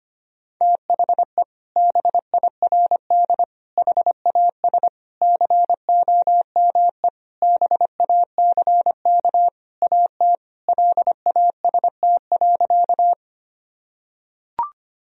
Morse code, heard as a keyed tone, about 25 wpm, 700 Hz. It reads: THE BIRD HAS COME BACK AT LAST.
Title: Sentences from Top 1000 Words - Review Part 2 - 25wpm